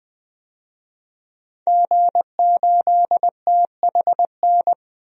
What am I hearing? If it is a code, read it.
Morse code, heard as a keyed tone, about 20 wpm, 700 Hz. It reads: G8THN
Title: DXCC-100 Call Signs - 5 Characters - Repeated 2x - 20wpm